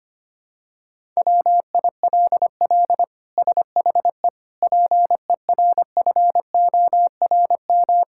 Morse code, heard as a keyed tone, about 25 wpm, 700 Hz. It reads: WILL SHE PERFORM
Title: Sentences from Top 1000 Words - Review Part 5 - 25wpm